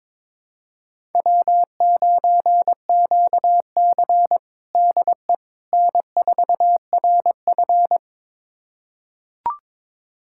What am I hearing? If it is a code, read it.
Morse code, heard as a keyed tone, about 22 wpm, 700 Hz. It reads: W9QC DE N4RF
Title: USA Callsign DE Callsign - 4 Characters - 22wpm